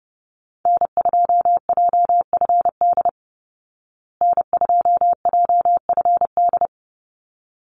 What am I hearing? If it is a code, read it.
Morse code, heard as a keyed tone, about 30 wpm, 700 Hz. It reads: N2JFD N2JFD
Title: Call Signs Repeated 3x - 30wpm